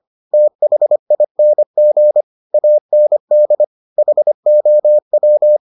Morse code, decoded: THING AND HOW

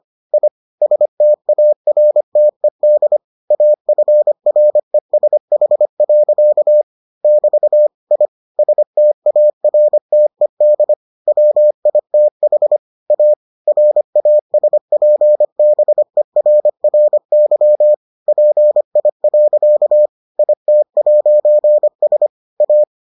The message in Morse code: I STARTED AFRESH. <BT> I STARTED WITH A RASPBERRY PI. IT'S A

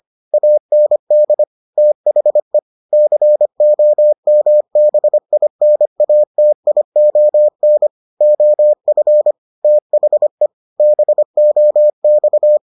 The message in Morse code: AND THE COMBINATION OF THE BOX

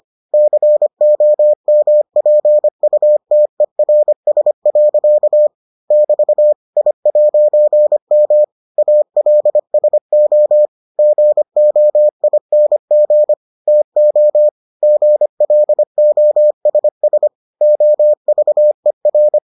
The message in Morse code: COMPUTERS. <BT> I'M ALSO GOING TO GLOSS OVER

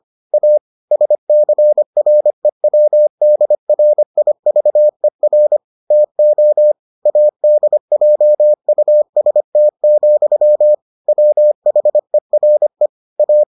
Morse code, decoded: A SCREWDRIVER TO ADJUST, WHERE A

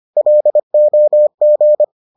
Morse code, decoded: LOG

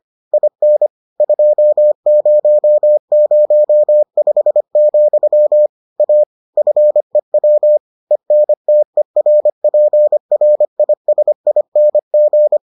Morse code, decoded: IN 2005, A FEW ENTERPRISING

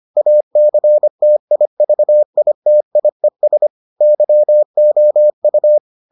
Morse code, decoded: ACTIVITIES YOU

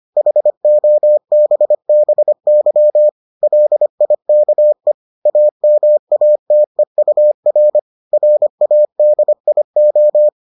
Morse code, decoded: HOBBY LIKE AMATEUR RADIO

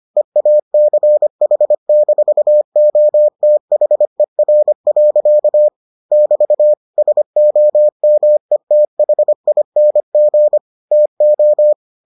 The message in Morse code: EACH-OTHER. <BT> SOMETHING TO

